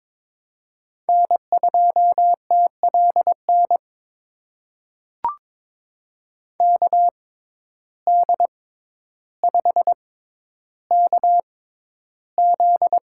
Morse code, heard as a keyed tone, about 21 wpm, spaced as for 11 wpm, 700 Hz. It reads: N2TLN KD5KZ
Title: USA Call Signs - 6x Character Spacing - 22wpm